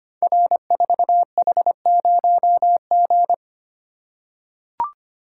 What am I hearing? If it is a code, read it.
Morse code, heard as a keyed tone, about 25 wpm, 700 Hz. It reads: R4H0G
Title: Code Groups - 5 Letter:Numbers - 25wpm